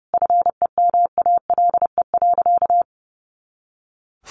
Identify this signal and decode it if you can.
Morse code, heard as a keyed tone, about 30 wpm, 700 Hz. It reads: FEMALE.
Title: Easy Phrases 30wpm